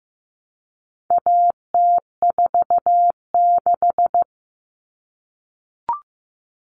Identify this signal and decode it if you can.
Morse code, heard as a keyed tone, about 15 wpm, 700 Hz. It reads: AT46
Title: All Cut Numbers - 4-Digit 15wpm